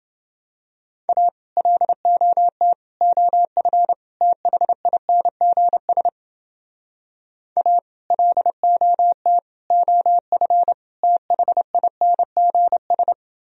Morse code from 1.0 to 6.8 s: A LOT OF THINGS

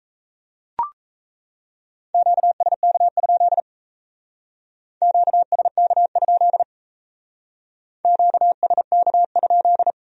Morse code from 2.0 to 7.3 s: QSK? QSK?

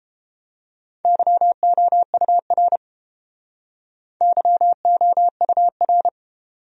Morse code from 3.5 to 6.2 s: YOUR

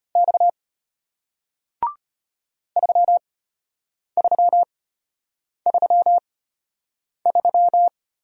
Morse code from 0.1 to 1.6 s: X

